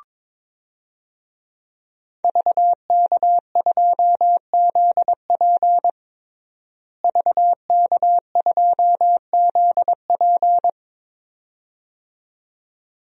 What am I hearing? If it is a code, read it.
Morse code, heard as a keyed tone, about 22 wpm, 700 Hz. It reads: VK2ZP VK2ZP
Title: DXCC-100 Call Signs - 5 Characters - Repeated 2x - 22wpm